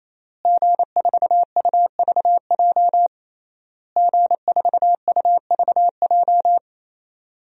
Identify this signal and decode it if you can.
Morse code, heard as a keyed tone, about 27 wpm, 700 Hz. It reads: G4UVJ G4UVJ
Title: DXCC-100 Call Signs - 5 Characters - Repeated 3x - 28wpm